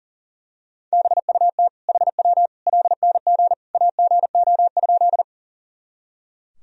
Morse code, heard as a keyed tone, about 37 wpm, 700 Hz. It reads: BUT HW LNG AGO?
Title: Sentences from Top 500 Words and Abbreviations 40wpm